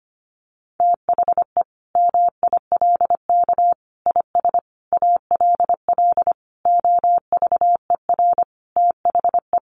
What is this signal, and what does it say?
Morse code, heard as a keyed tone, about 25 wpm, 700 Hz. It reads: THE MILK IS ALL OVER THE
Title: Additional Easy Phrases 25wpm